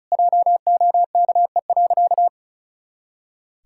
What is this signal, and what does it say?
Morse code, heard as a keyed tone, about 34 wpm, 700 Hz. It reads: JOKE.
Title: Pangram Challenge v2 - 35wpm